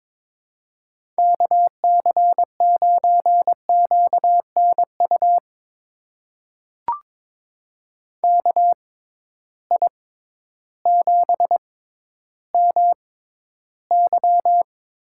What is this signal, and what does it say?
Morse code, heard as a keyed tone, about 22 wpm, spaced as for 12 wpm, 700 Hz. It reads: KC9QNU KI7MY
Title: USA Call Signs - 6 Characters - 6x Character Spacing - 22wpm